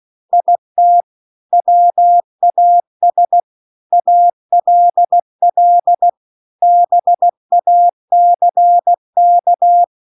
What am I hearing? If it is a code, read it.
Morse code, heard as a keyed tone, about 16 wpm, 700 Hz. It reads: IT WAS ALL BACK